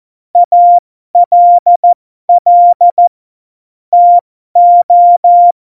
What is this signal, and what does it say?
Morse code, heard as a keyed tone, about 14 wpm, spaced as for 12 wpm, 700 Hz. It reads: ALL TO
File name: Morsekurs / Witze